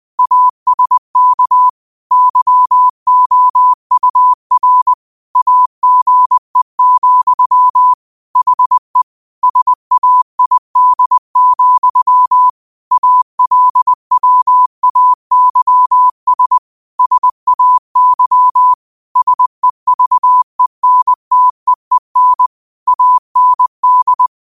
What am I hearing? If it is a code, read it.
Morse code, heard as a keyed tone, about 20 wpm, 1000 Hz. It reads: ASK YOUR AGE, HE SAID, ALWAYS SAY SEVENTEEN AND
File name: 20wpm